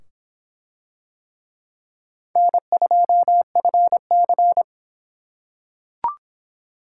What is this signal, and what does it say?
Morse code, heard as a keyed tone, about 25 wpm, 700 Hz. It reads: N2FC